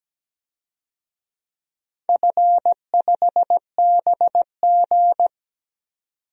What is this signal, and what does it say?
Morse code, heard as a keyed tone, about 17 wpm, 700 Hz. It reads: F5BG
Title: DXCC-100 Call Signs - 4 Characters - 17wpm